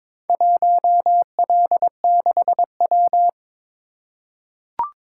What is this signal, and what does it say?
Morse code, heard as a keyed tone, about 22 wpm, 700 Hz. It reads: 1L6W
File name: Four Letter-Number Combinations - 22wpm